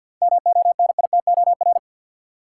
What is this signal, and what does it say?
Morse code, heard as a keyed tone, about 50 wpm, 700 Hz. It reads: MONITOR
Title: Seven Letter Words - 50wpm